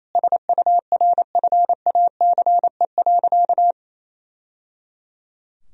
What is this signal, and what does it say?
Morse code, heard as a keyed tone, about 28 wpm, 700 Hz. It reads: SURFACE.
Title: Sentences from Top 1000 Words - Review Part 3 - 28wpm